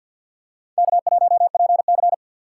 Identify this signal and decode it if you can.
Morse code, heard as a keyed tone, about 49 wpm, 700 Hz. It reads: K1PC
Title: USA Call Signs - 6x Character Spacing - 50wpm